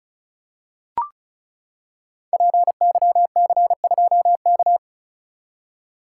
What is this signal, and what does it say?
Morse code, heard as a keyed tone, about 36 wpm, 700 Hz. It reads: PYC2K
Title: Code Groups - 5 Letter:Numbers - 35wpm